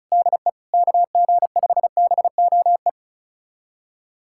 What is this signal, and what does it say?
Morse code, heard as a keyed tone, about 36 wpm, 690 Hz. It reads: DE KG5BOE
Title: USA Callsign DE Callsign - 6 Characters - 35wpm